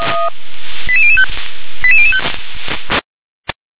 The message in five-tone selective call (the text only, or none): 88005 88005